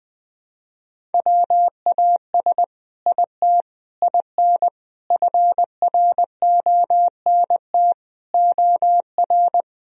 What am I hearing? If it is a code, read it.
Morse code, heard as a keyed tone, about 21 wpm, 700 Hz. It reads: WAS IT IN FRONT OR
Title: Sentences from Top 1000 Words - Review Part 2 - 20wpm